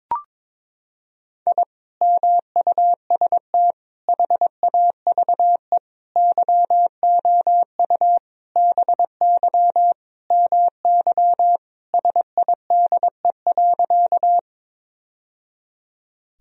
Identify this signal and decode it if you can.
Morse code, heard as a keyed tone, about 22 wpm, 700 Hz. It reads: I MUST HAVE YOU BY MY SIDE.
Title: Sentences from Top 1000 Words - Review Part 1 - 22wpm